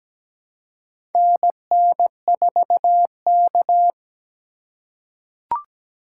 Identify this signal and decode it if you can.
Morse code, heard as a keyed tone, about 17 wpm, 700 Hz. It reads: NN4K
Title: USA Call Signs - 4 Characters - 4x Character Spacing - 17wpm